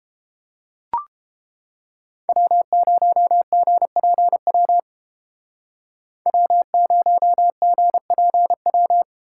Morse code, heard as a keyed tone, about 28 wpm, 700 Hz. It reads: W0GPW W0GPW